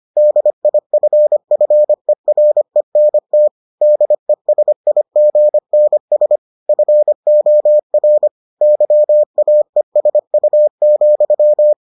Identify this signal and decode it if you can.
Morse code, heard as a keyed tone, about 25 wpm, 600 Hz. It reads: DIFFERENT DESIGNS FOR YAESU,